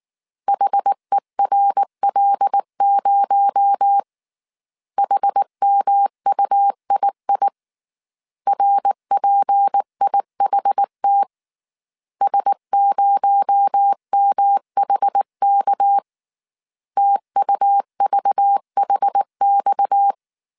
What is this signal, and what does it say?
Morse code, heard as a keyed tone, about 19 wpm, 800 Hz. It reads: HERL0 HMUII RPIHT S0MHK TUVHX
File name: CW / DET / 1 6 1500C S A 18wpm 2255